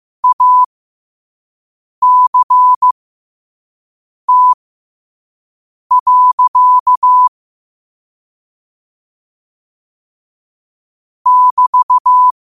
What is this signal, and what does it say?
Morse code, heard as a keyed tone, about 15 wpm, spaced as for 5 wpm, 1000 Hz. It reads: ACT. <BT>
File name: qotdfiles / 5wpm